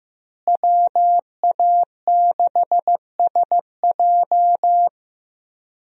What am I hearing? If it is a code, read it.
Morse code, heard as a keyed tone, about 15 wpm, 700 Hz. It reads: WA6SJ
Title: USA Call Signs - 5 Characters - Repeated 2x - 15wpm